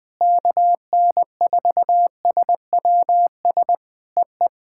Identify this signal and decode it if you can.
Morse code, heard as a keyed tone, about 20 wpm, 700 Hz. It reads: KN4SWS EE
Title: POTA QSO Practice - 20wpm